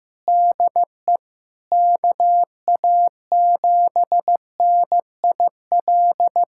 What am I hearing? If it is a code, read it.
Morse code, heard as a keyed tone, about 15 wpm, 700 Hz. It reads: DE KA7NIL
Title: USA Callsign DE Callsign 15wpm